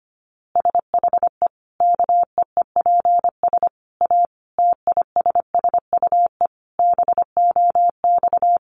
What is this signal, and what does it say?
Morse code, heard as a keyed tone, about 25 wpm, 700 Hz. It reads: SHE KEEPS A TISSUE BOX
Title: Easy Phrases 25wpm